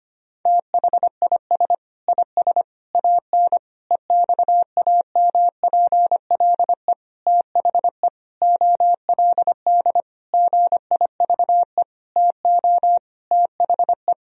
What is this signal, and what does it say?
Morse code, heard as a keyed tone, about 25 wpm, 700 Hz. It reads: THIS IS AN EXAMPLE THE OLD GIVE TO THE
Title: Sentences from Top 1000 Words - Review Part 1 - 25wpm